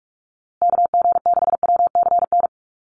Morse code, heard as a keyed tone, about 44 wpm, 690 Hz. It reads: KG6WCN
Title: USA Call Signs Repeated 3x 45wpm